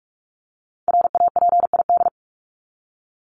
Morse code, heard as a keyed tone, about 44 wpm, 700 Hz. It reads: RAPID
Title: LY Suffix - Top 100 Words with Base Words 45wpm